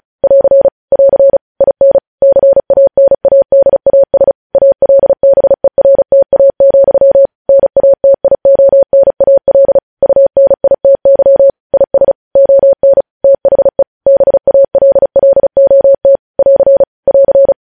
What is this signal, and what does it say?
Morse code, heard as a keyed tone, about 36 wpm, 550 Hz. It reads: <AR> <AR> IN CANADAS ALBERTA, NATIONAL UNITY IS ON THE BALLOT <AR> <AR>
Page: News headlines in Morse code at 35 wpm